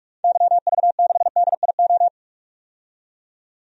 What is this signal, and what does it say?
Morse code, heard as a keyed tone, about 46 wpm, 700 Hz. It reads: YV6DIO
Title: DXCC-100 Call Signs - 6 Characters - 45wpm